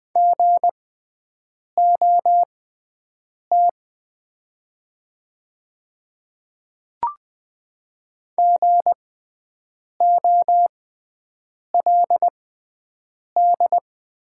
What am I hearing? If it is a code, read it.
Morse code, heard as a keyed tone, about 20 wpm, spaced as for 7 wpm, 700 Hz. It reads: GOT GOLD
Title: 6x Character Spacing - Top 500 Words - 20wpm